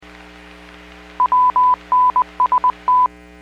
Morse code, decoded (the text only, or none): WNST